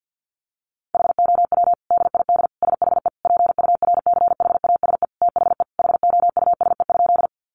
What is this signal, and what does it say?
Morse code, heard as a keyed tone, about 50 wpm, 700 Hz. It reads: HOW DID SHE PURCHASE THE HOUSE?
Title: Sentences from Top 800 Words 50wpm